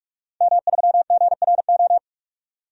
Morse code, heard as a keyed tone, about 46 wpm, 700 Hz. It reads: M3GRO
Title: DXCC-100 Call Signs - 5 Characters - 45wpm